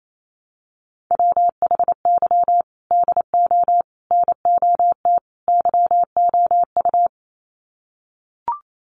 Morse code, heard as a keyed tone, about 28 wpm, 700 Hz. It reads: WHY DO NOT YOU
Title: Sets of 4 Words 28wpm